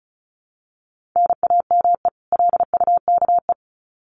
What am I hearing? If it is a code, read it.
Morse code, heard as a keyed tone, about 34 wpm, 700 Hz. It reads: NAME LUKE
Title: Common QSO Phrases 35wpm